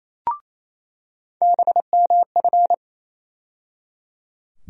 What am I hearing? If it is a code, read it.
Morse code, heard as a keyed tone, about 28 wpm, 700 Hz. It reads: BMF
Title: USA Call Sign Postfixes - 3 Letters 28wpm